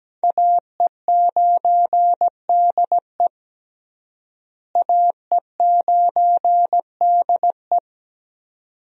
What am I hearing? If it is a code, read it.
Morse code, heard as a keyed tone, about 17 wpm, 700 Hz. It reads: AE9DE AE9DE
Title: USA Call Signs - 5 Characters - Repeated 2x - 17wpm